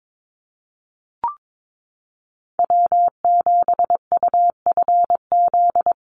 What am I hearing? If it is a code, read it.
Morse code, heard as a keyed tone, about 22 wpm, 700 Hz. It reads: W7UFZ
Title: USA Call Signs Repeated 3x 22wpm